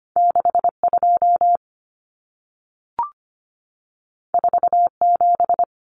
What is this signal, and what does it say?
Morse code, heard as a keyed tone, about 25 wpm, 700 Hz. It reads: 62 47
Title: Numbers - 2-Digit 25wpm